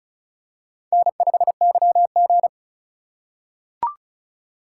Morse code, heard as a keyed tone, about 36 wpm, 700 Hz. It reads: N5YG
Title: USA Call Signs - 4 Characters - 3x Character Spacing - 35wpm